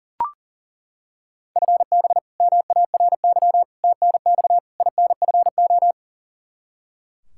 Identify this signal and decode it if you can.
Morse code, heard as a keyed tone, about 40 wpm, 700 Hz. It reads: FB MARY TNX INFO
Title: Practice QSO - Exchange 3 - More Info - 40wpm